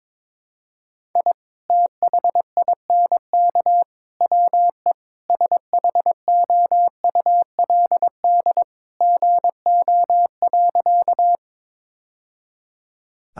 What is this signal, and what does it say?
Morse code, heard as a keyed tone, about 22 wpm, 700 Hz. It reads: I THINK WE SHOULD GO.
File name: practice / 22wpm / Sentences with 3-Word Phrases - 22wpm